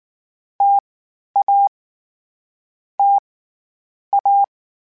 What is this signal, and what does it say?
Morse code, heard as a keyed tone, about 19 wpm, 800 Hz. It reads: T A T A
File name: Variable Speed / Variable Speed-mp3s / T AOEI's sp @18